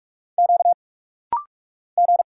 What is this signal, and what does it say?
Morse code, heard as a keyed tone, about 44 wpm, 700 Hz. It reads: Q G
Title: Single Letter-Number Alt - Rapid-Fire 45wpm